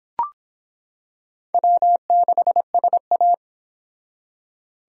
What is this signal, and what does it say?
Morse code, heard as a keyed tone, about 26 wpm, 700 Hz. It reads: W6SA